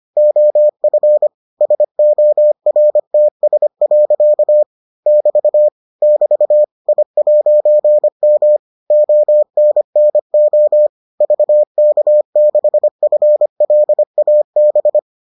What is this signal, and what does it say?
Morse code, heard as a keyed tone, about 25 wpm, 600 Hz. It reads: OF SORTS. <BT> <BT> I'M ONNO VK6FLAB